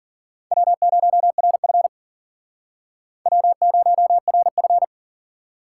Morse code, heard as a keyed tone, about 51 wpm, 700 Hz. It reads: W0RF W0RF